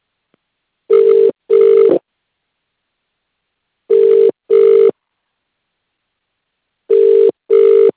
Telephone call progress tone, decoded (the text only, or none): ringback tone